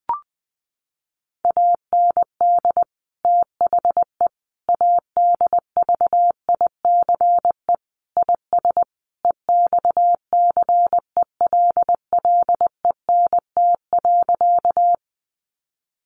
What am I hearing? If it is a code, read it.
Morse code, heard as a keyed tone, about 19 wpm, 700 Hz. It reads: AND THE ADVICE IS EXCELLENT.
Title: Sentences from Top 1100 Words 20wpm